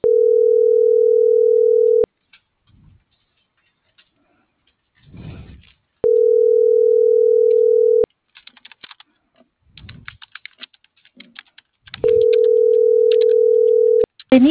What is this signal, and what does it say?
Ringback tone (North America), 440 + 480 Hz.